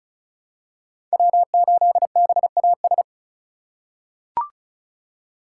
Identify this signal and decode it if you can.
Morse code, heard as a keyed tone, about 36 wpm, 700 Hz. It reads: W8BAS
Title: USA Call Signs - 2x Character Spacing - 35wpm